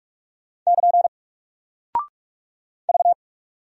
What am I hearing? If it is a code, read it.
Morse code, heard as a keyed tone, about 44 wpm, 700 Hz. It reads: <KN> V
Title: Single Character (Letter:Number:Punctuation:Prosign) Alt 45wpm